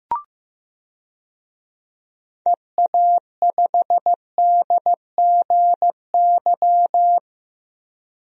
Morse code, heard as a keyed tone, about 15 wpm, 700 Hz. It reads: EA5DGY
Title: DXCC-100 Call Signs - 6 Characters - 15wpm